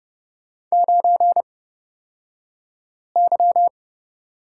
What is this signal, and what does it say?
Morse code, heard as a keyed tone, about 30 wpm, spaced as for 5 wpm, 700 Hz. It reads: 9Y
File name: Lesson 36d - BT - Two Characters 30wpm